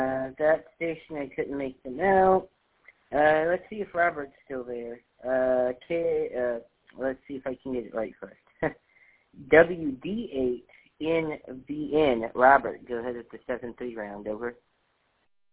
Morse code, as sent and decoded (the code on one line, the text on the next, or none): none